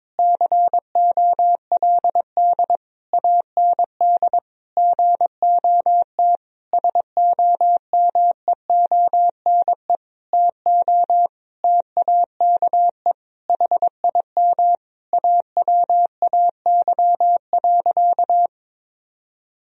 Morse code, heard as a keyed tone, about 22 wpm, 700 Hz. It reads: COLD AND GOT SOMEONE TO TAKE HIM AWAY.
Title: Sentences from Top 1000 Words - Review Part 4 - 22wpm